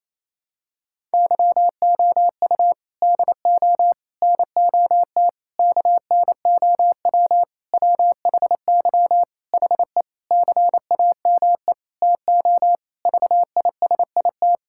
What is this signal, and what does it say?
Morse code, heard as a keyed tone, about 28 wpm, 700 Hz. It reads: YOU DO NOT KNOW WHY HE CAME TO VISIT
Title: Sentences from Top 1000 Words - Review Part 6 - 28wpm